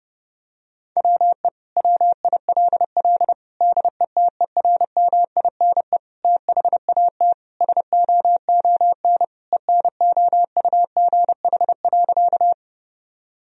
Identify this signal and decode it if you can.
Morse code, heard as a keyed tone, about 30 wpm, 700 Hz. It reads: WE WILL DETERMINE THAT SOON ENOUGH.
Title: Sentences from Top 1000 Words - Review Part 4 - 30wpm